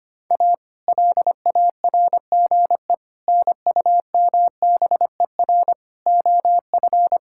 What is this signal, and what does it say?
Morse code, heard as a keyed tone, about 25 wpm, 700 Hz. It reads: A LARGE NUMBER OF
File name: practice / 25wpm / Sentences with 4-Word Phrases - 25wpm